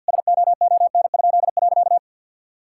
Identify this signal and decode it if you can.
Morse code, heard as a keyed tone, about 50 wpm, 700 Hz. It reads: SOON?.
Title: Pangram Challenge v2 - 50wpm